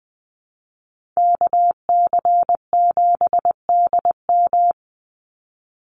Morse code, heard as a keyed tone, about 20 wpm, 700 Hz. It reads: KC7DM
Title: NA SOTA Activator Callsigns 20wpm